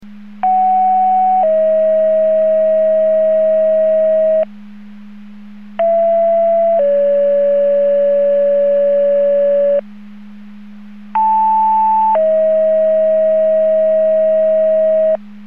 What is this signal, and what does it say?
Two-tone page, heard as a one-second tone, then a three-second tone, about 750 then 640 Hz, about 670 then 570 Hz, about 880 then 640 Hz. It is heard from 0.4 to 15.2 s.